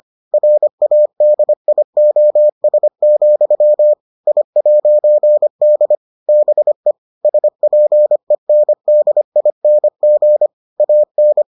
Morse code, heard as a keyed tone, about 25 wpm, 600 Hz. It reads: RADIOS, I'D BE SPENDING AN